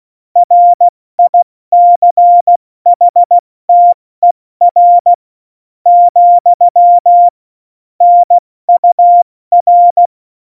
Morse code, heard as a keyed tone, about 16 wpm, spaced as for 13 wpm, 700 Hz. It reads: RICHTER, NUR
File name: Morsekurs / Witze / Witz 05. Januar 2021, 70 F80